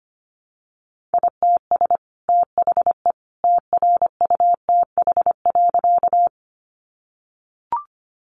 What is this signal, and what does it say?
Morse code, heard as a keyed tone, about 25 wpm, 700 Hz. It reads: ITS THE TRUTH.
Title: Sentences from Top 800 Words 25wpm